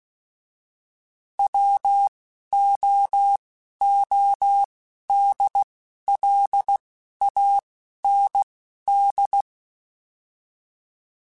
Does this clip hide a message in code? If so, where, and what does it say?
Yes: Morse code from 1.4 to 9.4 s, heard as a keyed tone, about 16 wpm, spaced as for 11 wpm, 800 Hz. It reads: WOODLAND